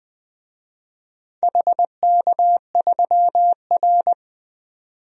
Morse code, heard as a keyed tone, about 20 wpm, 700 Hz. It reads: HK3R
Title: DXCC-100 Call Signs - 4 Characters - 20wpm